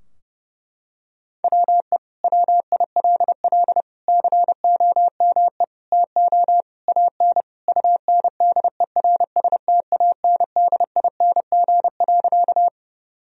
Morse code, heard as a keyed tone, about 32 wpm, 700 Hz. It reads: WE WILL COME TO AN UNDERSTANDING.
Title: Sentences from Top 1000 Words - Review Part 3 - 30wpm